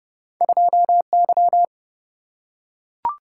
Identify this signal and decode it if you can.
Morse code, heard as a keyed tone, about 30 wpm, 700 Hz. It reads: 2Y
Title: Single-Letter Single-Number 30wpm